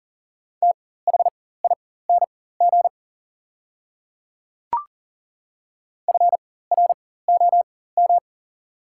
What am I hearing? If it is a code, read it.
Morse code, heard as a keyed tone, about 40 wpm, spaced as for 19 wpm, 690 Hz. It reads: THING FROM
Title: 4x Character Spacing - Top 100 Words - 40wpm